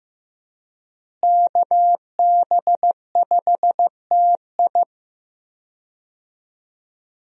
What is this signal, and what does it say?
Morse code, heard as a keyed tone, about 15 wpm, 700 Hz. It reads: KB5TI
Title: USA Call Signs - 5 Characters - Repeated 2x - 15wpm